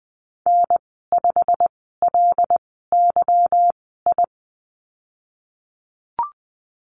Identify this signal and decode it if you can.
Morse code, heard as a keyed tone, about 20 wpm, spaced as for 15 wpm, 700 Hz. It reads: N5LYI
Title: 173 Lesson 28g - Number 9 - Call Signs 20wpm